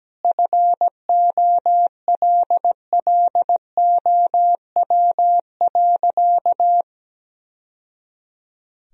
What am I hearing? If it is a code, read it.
Morse code, heard as a keyed tone, about 17 wpm, 700 Hz. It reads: FOLLOW.
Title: Sentences from Top 1000 Words - Review Part 2 - 17wpm